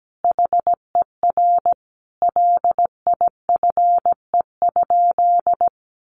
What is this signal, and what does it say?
Morse code, heard as a keyed tone, about 17 wpm, 700 Hz. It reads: HER LIFE?
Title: Sentences from Top 1500 Words 17wpm